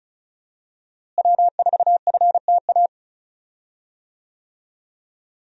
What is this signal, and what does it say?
Morse code, heard as a keyed tone, about 34 wpm, 700 Hz. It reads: W4FTA
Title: USA Call Signs - 5 Characters - Repeated 2x - 35wpm